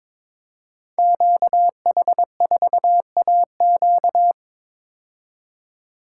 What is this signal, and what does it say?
Morse code, heard as a keyed tone, about 22 wpm, 700 Hz. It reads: QH4AQ
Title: Code Groups - 5 Letter:Numbers - 22wpm